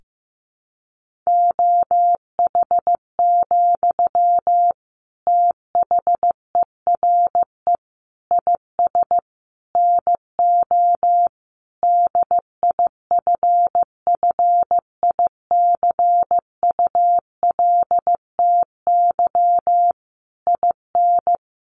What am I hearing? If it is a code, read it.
Morse code, heard as a keyed tone, about 14 wpm, 700 Hz. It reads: OH, THERE IS NO DIFFICULTY IN
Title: Sentences from Top 1500 Words 15wpm